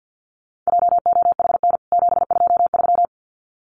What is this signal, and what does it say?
Morse code, heard as a keyed tone, about 50 wpm, 700 Hz. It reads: JOHN 723
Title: CWT Exchange 50wpm